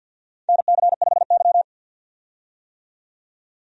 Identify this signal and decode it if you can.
Morse code, heard as a keyed tone, about 52 wpm, 700 Hz. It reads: NC5Y
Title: USA Call Signs - 4 Characters - 50wpm